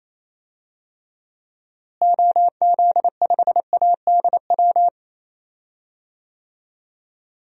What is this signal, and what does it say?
Morse code, heard as a keyed tone, about 28 wpm, 700 Hz. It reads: OZ5ADW
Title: DXCC-100 Call Signs - 6 Characters - Repeated 3x - 28wpm